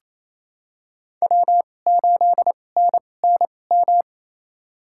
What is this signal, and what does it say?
Morse code, heard as a keyed tone, about 28 wpm, spaced as for 20 wpm, 700 Hz. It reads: W8NNM
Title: USA Call Signs - 5 Characters - 2x Character Spacing - 28wpm